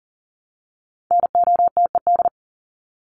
Morse code, heard as a keyed tone, about 40 wpm, 700 Hz. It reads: NOTED
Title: ED Suffix - Top 100 Words with Base Words 40wpm